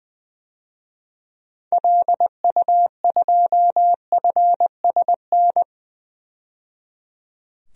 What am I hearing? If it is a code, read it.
Morse code, heard as a keyed tone, about 20 wpm, 700 Hz. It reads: LU2FSN
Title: DXCC-100 Call Signs - 6 Characters - 20wpm